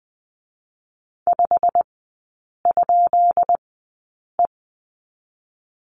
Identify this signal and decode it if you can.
Morse code, heard as a keyed tone, about 20 wpm, spaced as for 8 wpm, 700 Hz. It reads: 5?E
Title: 218 Lesson 35e - Number 0 - Three Characters 20wpm